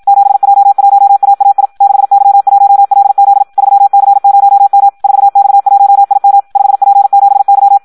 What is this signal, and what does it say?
Morse code, heard as a keyed tone, about 56 wpm, 800 Hz. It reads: 7J1AAI 6Y1LZ 3C0M 4X1IM 5P7Y
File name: Audio practica CW / qrq-tx-54 2